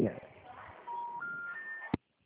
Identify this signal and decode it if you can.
Special information tone, with speech.
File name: out- -2016- - - .4071